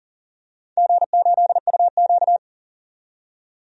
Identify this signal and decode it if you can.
Morse code, heard as a keyed tone, about 40 wpm, 700 Hz. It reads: G8UQ
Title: DXCC-100 Call Signs - 4 Characters - 40wpm